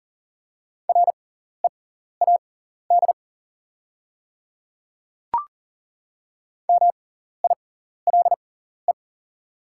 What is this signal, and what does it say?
Morse code, heard as a keyed tone, about 40 wpm, spaced as for 14 wpm, 700 Hz. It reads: READ MILE
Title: 6x Character Spacing - Top 300 Words - 40wpm